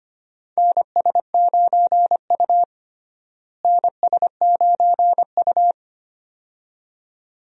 Morse code, heard as a keyed tone, about 25 wpm, 700 Hz. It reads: NS9U NS9U
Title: USA Call Signs - 4 Characters - Repeated 2x - 25wpm